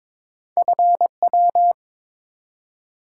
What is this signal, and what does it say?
Morse code, heard as a keyed tone, about 22 wpm, 700 Hz. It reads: FW
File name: practice / 22wpm / Two Letter-Number Combinations - 22wpm